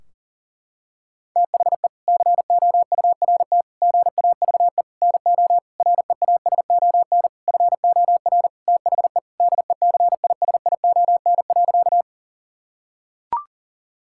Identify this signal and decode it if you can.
Morse code, heard as a keyed tone, about 40 wpm, 700 Hz. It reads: THE COURT GAVE NO REASON FOR THE DECISION.
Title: Sentences from Top 1600 Words 40wpm